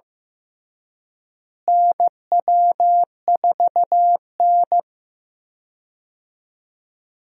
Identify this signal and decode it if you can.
Morse code, heard as a keyed tone, about 15 wpm, 700 Hz. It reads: NW4N
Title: USA Call Signs - 4 Characters - Repeated 2x - 15wpm